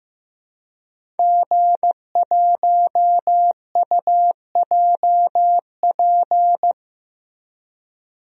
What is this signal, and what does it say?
Morse code, heard as a keyed tone, about 15 wpm, 700 Hz. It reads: G1UJP
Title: DXCC-100 Call Signs - 5 Characters - Repeated 3x - 15wpm